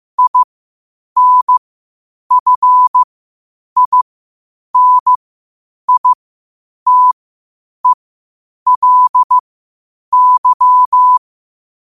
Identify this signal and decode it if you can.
Morse code, heard as a keyed tone, about 15 wpm, spaced as for 8 wpm, 1000 Hz. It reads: INFINITELY
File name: qotdfiles / 8wpm